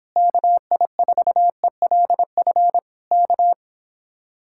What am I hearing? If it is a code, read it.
Morse code, heard as a keyed tone, about 26 wpm, 700 Hz. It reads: KI4ELF K